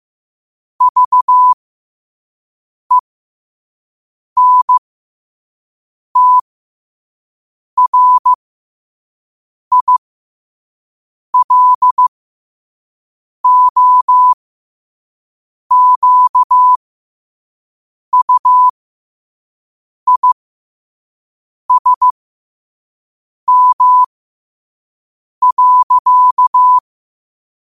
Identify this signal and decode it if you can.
Morse code, heard as a keyed tone, about 15 wpm, spaced as for 5 wpm, 1000 Hz. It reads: VENTRILOQUISM.
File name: qotdfiles / 5wpm